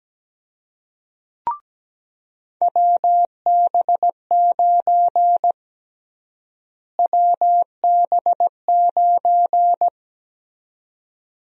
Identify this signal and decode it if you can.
Morse code, heard as a keyed tone, about 17 wpm, 700 Hz. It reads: WB9 WB9
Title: USA Call Sign Prefixes - Repeated 2x - 17wpm